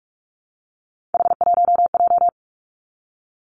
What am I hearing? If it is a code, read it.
Morse code, heard as a keyed tone, about 46 wpm, 700 Hz. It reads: H1J
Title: Three Challenging Letter-Number Combinations 45wpm